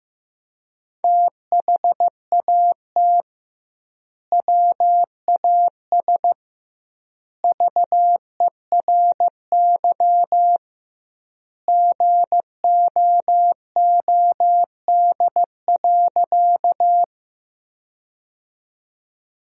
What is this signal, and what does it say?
Morse code, heard as a keyed tone, about 15 wpm, 700 Hz. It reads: THAT WAS VERY GOOD.
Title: 2x Word-Spacing - Sentences from Top 200 Words - 15wpm